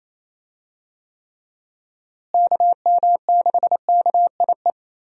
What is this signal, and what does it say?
Morse code, heard as a keyed tone, about 28 wpm, 700 Hz. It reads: KM6KIE
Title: USA Call Signs - 6 Characters - Repeated 3x - 28wpm